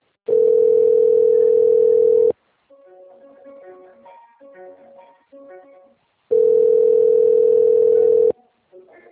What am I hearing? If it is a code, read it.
Ringback tone (North America), 440 + 480 Hz.
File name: rg-1600-unknown- - - .14446